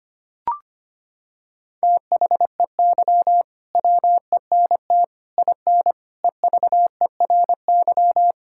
Morse code, heard as a keyed tone, about 25 wpm, 700 Hz. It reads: THEY WENT IN EVERY
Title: Sentences from Top 1000 Words - Review Part 5 - 25wpm